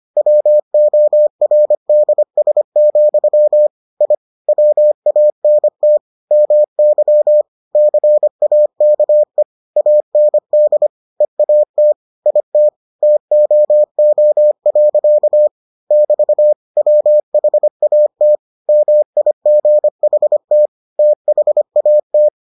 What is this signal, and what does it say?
Morse code, heard as a keyed tone, about 25 wpm, 600 Hz. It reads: WORDS, I WANT MY CAKE AND EAT IT TOO. <BT> WHAT MIGHT THAT